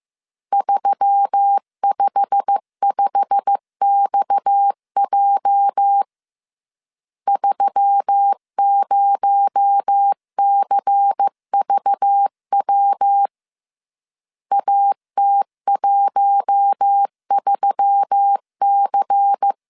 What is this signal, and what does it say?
Morse code, heard as a keyed tone, about 15 wpm, 800 Hz. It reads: 355XJ 30CVW AT13C
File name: CW / DET / 1 9 1500C S C 14wpm 3245